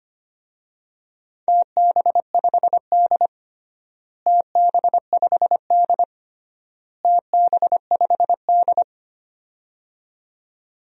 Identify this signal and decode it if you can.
Morse code, heard as a keyed tone, about 25 wpm, 700 Hz. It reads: TB5D TB5D TB5D
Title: DXCC-100 Call Signs - 4 Characters - Repeated 3x - 25wpm